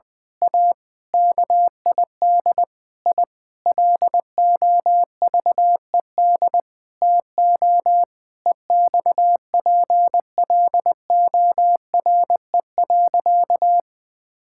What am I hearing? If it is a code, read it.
Morse code, heard as a keyed tone, about 20 wpm, 700 Hz. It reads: A KID I LOVED TO EXPLORE.
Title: Sentences with 6-Word Phrases - 20wpm